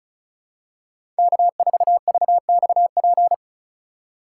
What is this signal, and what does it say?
Morse code, heard as a keyed tone, about 34 wpm, 700 Hz. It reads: K4VXP
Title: USA Call Signs - 5 Characters - 4x Character Spacing - 35wpm